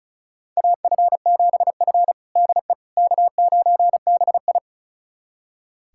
Morse code, heard as a keyed tone, about 36 wpm, 700 Hz. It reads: AF7F DE K9BI
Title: USA Callsign DE Callsign - 4 Characters - 35wpm